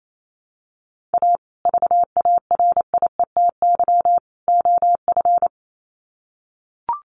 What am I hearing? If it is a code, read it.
Morse code, heard as a keyed tone, about 28 wpm, 700 Hz. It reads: A VARIETY OF
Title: Sets of 3 Words 28wpm